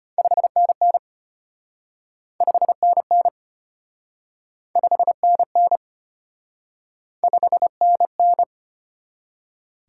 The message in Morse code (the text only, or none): 5NN 5NN 5NN 5NN